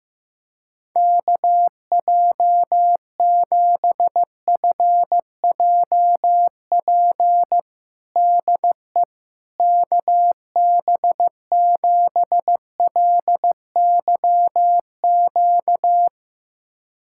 Morse code, heard as a keyed tone, about 16 wpm, 700 Hz. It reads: KJ7FJP DE KB7LYQ
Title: USA Callsign DE Callsign - 6 Characters - 15wpm